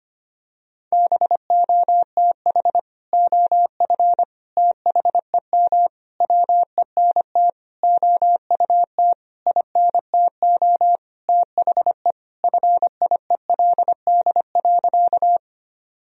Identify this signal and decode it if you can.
Morse code, heard as a keyed tone, about 25 wpm, 700 Hz. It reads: BOTH OF THEM WENT OUT INTO THE FIELD.
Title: Sentences from Top 1000 Words - Review Part 2 - 25wpm